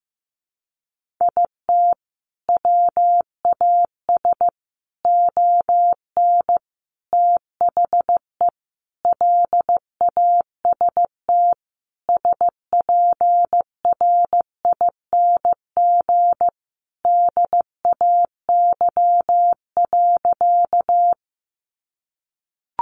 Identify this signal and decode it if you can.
Morse code, heard as a keyed tone, about 15 wpm, 700 Hz. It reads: IT WAS ON THE LAST SPRING DAY.
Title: Sentences from Top 1200 Words 15wpm